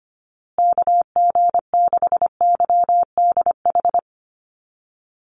Morse code, heard as a keyed tone, about 25 wpm, 700 Hz. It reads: KG6YDH
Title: USA Call Signs Repeated 3x 25wpm